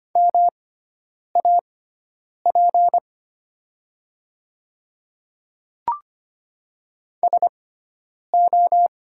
Morse code, heard as a keyed tone, about 25 wpm, spaced as for 9 wpm, 700 Hz. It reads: MAP SO
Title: 6x Character Spacing - Top 500 Words - 25wpm